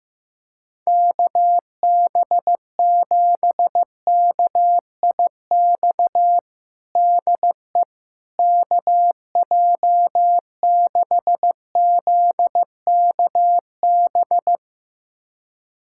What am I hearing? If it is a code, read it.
Morse code, heard as a keyed tone, about 15 wpm, 700 Hz. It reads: KB7KIX DE KJ6ZKB
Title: USA Callsign DE Callsign 15wpm